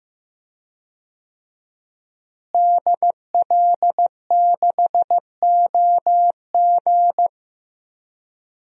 Morse code, heard as a keyed tone, about 15 wpm, 700 Hz. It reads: DL6OG
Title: DXCC-100 Call Signs - 5 Characters - Repeated 2x - 15wpm